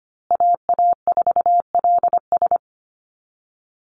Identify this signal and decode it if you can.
Morse code, heard as a keyed tone, about 25 wpm, 700 Hz. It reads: AA4LS
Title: Call Signs Repeated 2x - 25wpm